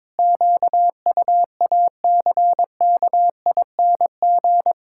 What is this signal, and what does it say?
Morse code, heard as a keyed tone, about 22 wpm, 700 Hz. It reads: QUACKING
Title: Pangram Challenge v2 - 22wpm